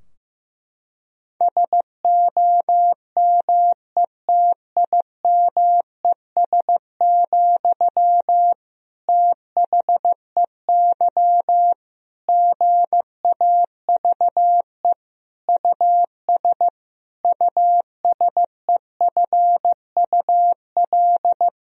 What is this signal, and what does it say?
Morse code, heard as a keyed tone, about 15 wpm, 700 Hz. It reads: SOMETIMES, THEY GAVE US USEFUL
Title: Sentences from Top 1000 Words - Review Part 4 - 15wpm